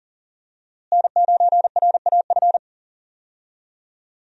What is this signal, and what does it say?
Morse code, heard as a keyed tone, about 39 wpm, 700 Hz. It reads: N9RAF
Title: USA Call Signs - 5 Characters - Repeated 2x - 40wpm